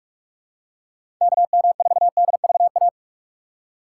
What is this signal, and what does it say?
Morse code, heard as a keyed tone, about 46 wpm, 700 Hz. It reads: KM4DVA
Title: USA Call Signs - 6 Characters - 2x Character Spacing - 45wpm